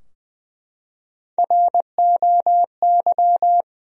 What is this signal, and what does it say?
Morse code, heard as a keyed tone, about 20 wpm, 700 Hz. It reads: ROY
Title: USA Common Names - Max 3 Letters - 20wpm